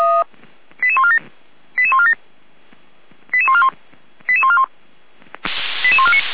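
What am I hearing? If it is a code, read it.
Five-tone selective call (ZVEI), heard as quick runs of five tones, tones of about 70 ms. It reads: 88147 88147 88141 88141 88149
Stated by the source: ZVEI five-tone call 88149 received by SDR.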